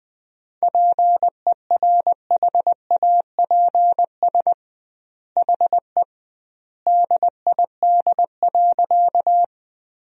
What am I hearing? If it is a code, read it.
Morse code, heard as a keyed tone, about 20 wpm, 700 Hz. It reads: PERHAPS HE DID.
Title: 2x Word-Spacing - Sentences from Top 500 Words - 20wpm